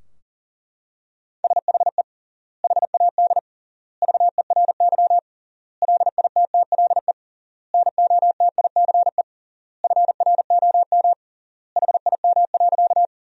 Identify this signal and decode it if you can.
Morse code, heard as a keyed tone, about 37 wpm, 700 Hz. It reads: SHE HAD VERY LITTLE NOTICE FROM HIM.
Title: 3x Word-Spacing - Sentences from Top 400 Words - 40wpm